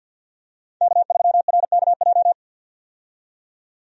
Morse code, heard as a keyed tone, about 53 wpm, 700 Hz. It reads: K3RKJ
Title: USA Call Signs - 5 Characters - 50wpm